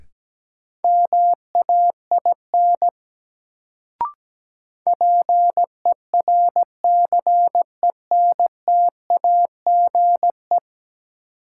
Morse code, heard as a keyed tone, about 17 wpm, 700 Hz. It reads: MAIN PERCENTAGE
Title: Sentences from Top 1300 Words - New Word Drill Alt - 17wpm